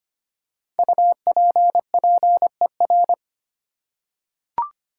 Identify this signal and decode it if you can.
Morse code, heard as a keyed tone, about 25 wpm, 700 Hz. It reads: UPPER
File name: 5 Letter Words - First 3 Letters Given - 25wpm